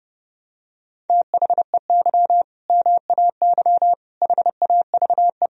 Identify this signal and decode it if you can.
Morse code, heard as a keyed tone, about 32 wpm, 700 Hz. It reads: THEY MAY HAVE
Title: Sentences from Top 1000 Words - Review Part 2 - 30wpm